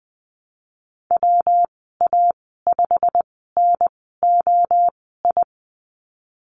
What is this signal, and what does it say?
Morse code, heard as a keyed tone, about 19 wpm, spaced as for 14 wpm, 700 Hz. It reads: WA5NOI
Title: 238 Lesson 38g - Letter q - Call Signs 20wpm